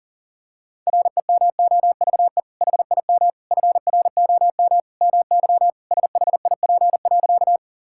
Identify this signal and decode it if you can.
Morse code, heard as a keyed tone, about 40 wpm, 700 Hz. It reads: REMOVE HIM FROM MY SHIP.
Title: Sentences from Top 1000 Words - Review Part 5 - 40wpm